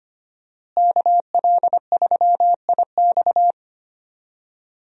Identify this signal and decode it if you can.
Morse code, heard as a keyed tone, about 25 wpm, 700 Hz. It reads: KL3IX
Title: USA Call Signs - 5 Characters - Repeated 3x - 25wpm